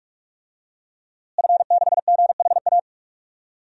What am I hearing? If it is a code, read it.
Morse code, heard as a keyed tone, about 46 wpm, 700 Hz. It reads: F6GHA
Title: DXCC-100 Call Signs - 5 Characters - Repeated 2x - 45wpm